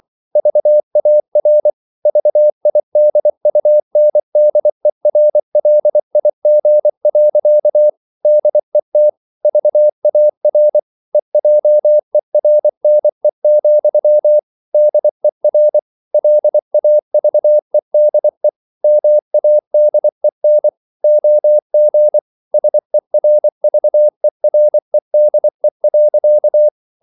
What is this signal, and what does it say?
Morse code, heard as a keyed tone, about 24 wpm, 600 Hz. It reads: VAR VIDUNDERLIG. DET VAR EJERNE, DER LAVEDE MADEN OG SERVEREDE.
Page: Ferie 24 wpm